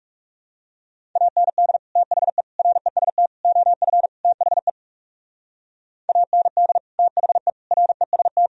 Morse code, heard as a keyed tone, about 39 wpm, 700 Hz. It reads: AND THE REST OF THE AND THE REST